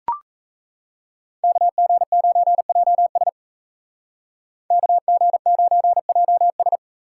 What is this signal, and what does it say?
Morse code, heard as a keyed tone, about 36 wpm, 700 Hz. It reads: KG9JS KG9JS